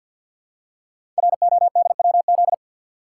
Morse code, heard as a keyed tone, about 50 wpm, 700 Hz. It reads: RODWZ
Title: Code Groups - 5 Letters - 50wpm